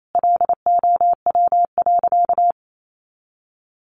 Morse code, heard as a keyed tone, about 28 wpm, 700 Hz. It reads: LOW.
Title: Sentences from Top 500 Words 28wpm